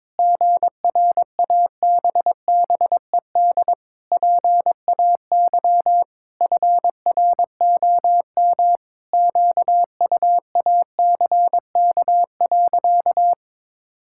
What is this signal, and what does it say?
Morse code, heard as a keyed tone, about 22 wpm, 700 Hz. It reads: GRABBED PAY FROM QUACK.
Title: Pangram Challenge v2 - 22wpm